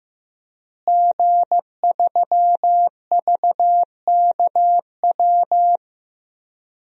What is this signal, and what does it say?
Morse code, heard as a keyed tone, about 15 wpm, 700 Hz. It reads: G3VKW
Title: DXCC-100 Call Signs - 5 Characters - Repeated 2x - 15wpm